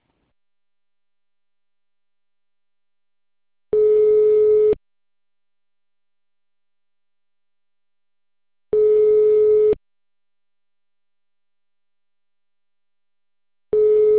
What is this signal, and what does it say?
Ringback tone (Europe, 425 Hz).